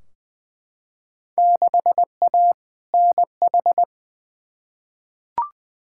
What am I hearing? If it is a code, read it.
Morse code, heard as a keyed tone, about 20 wpm, 700 Hz. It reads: 6A NH
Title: Field Day Exchanges - 20wpm